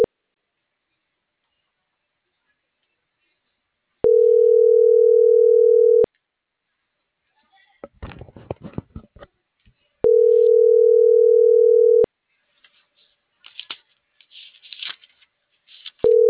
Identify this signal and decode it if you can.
Ringback tone (North America), 440 + 480 Hz.